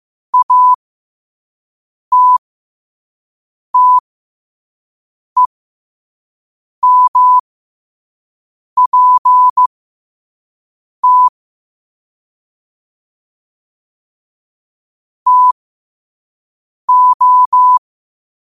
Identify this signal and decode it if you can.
Morse code, heard as a keyed tone, about 16 wpm, spaced as for 5 wpm, 1000 Hz. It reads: ATTEMPT TO
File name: qotdfiles / 5wpm